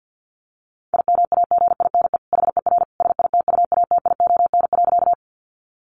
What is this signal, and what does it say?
Morse code, heard as a keyed tone, about 48 wpm, 700 Hz. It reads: IMAGINE HER SITUATION.
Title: Sentences from Top 1400 Words 50wpm